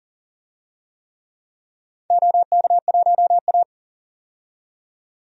Morse code, heard as a keyed tone, about 40 wpm, 700 Hz. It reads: OK1A
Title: DXCC-100 Call Signs - 4 Characters - 40wpm